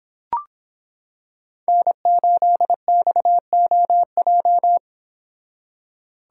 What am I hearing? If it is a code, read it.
Morse code, heard as a keyed tone, about 26 wpm, 700 Hz. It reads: N8XOJ